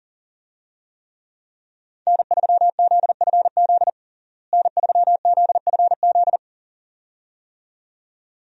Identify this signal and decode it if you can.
Morse code, heard as a keyed tone, about 40 wpm, 700 Hz. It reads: N3ZFZ N3ZFZ
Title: USA Call Signs - 5 Characters - Repeated 2x - 40wpm